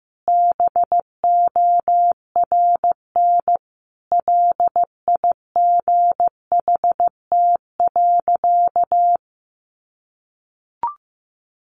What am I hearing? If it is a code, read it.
Morse code, heard as a keyed tone, about 15 wpm, 700 Hz. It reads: BORN LIGHT.
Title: Sentences from Top 1100 Words 15wpm